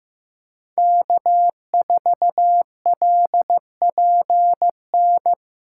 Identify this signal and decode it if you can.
Morse code, heard as a keyed tone, about 16 wpm, 700 Hz. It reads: K4LPN
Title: USA Call Signs - 5 Characters - 2x Character Spacing - 15wpm